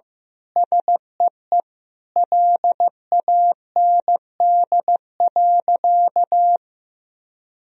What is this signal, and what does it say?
Morse code, heard as a keyed tone, about 16 wpm, 700 Hz. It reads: SEE LAND.
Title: Sentences from Top 1000 Words - Review Part 2 - 15wpm